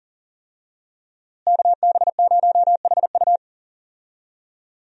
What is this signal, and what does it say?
Morse code, heard as a keyed tone, about 40 wpm, 700 Hz. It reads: KB0HU
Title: USA Call Signs - 5 Characters - Repeated 2x - 40wpm